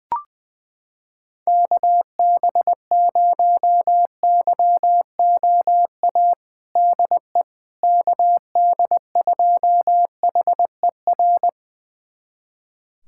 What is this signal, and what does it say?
Morse code, heard as a keyed tone, about 20 wpm, 700 Hz. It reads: KB0YOA DE KD2HER
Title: USA Callsign DE Callsign - 6 Characters - 20wpm